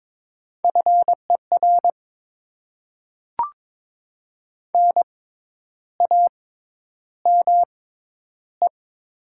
Morse code, heard as a keyed tone, about 21 wpm, spaced as for 10 wpm, 700 Hz. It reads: FER NAME
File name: Top 20 QSO Elements - 6x Character Spacing - 22wpm